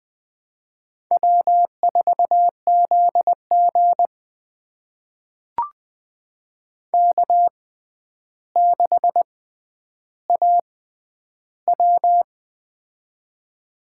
Morse code, heard as a keyed tone, about 21 wpm, spaced as for 10 wpm, 700 Hz. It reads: W4ZG K6AW
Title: USA Call Signs - 4 Characters - 6x Character Spacing - 20wpm